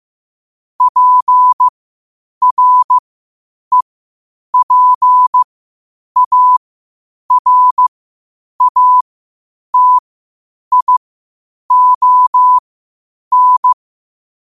Morse code, heard as a keyed tone, about 15 wpm, spaced as for 8 wpm, 1000 Hz. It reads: PREPARATION
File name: qotdfiles / 8wpm